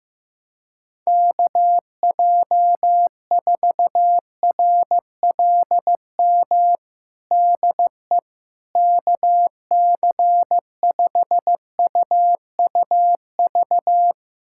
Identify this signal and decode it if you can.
Morse code, heard as a keyed tone, about 15 wpm, 700 Hz. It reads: KJ4RLM DE KC5UUV
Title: USA Callsign DE Callsign - 6 Characters - 15wpm